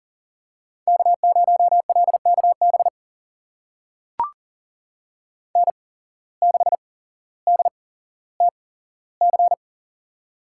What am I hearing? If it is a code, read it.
Morse code, heard as a keyed tone, about 37 wpm, spaced as for 17 wpm, 700 Hz. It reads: K0LKB N6DTC
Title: USA Call Signs - 5 Characters - 8x Character Spacing - 40wpm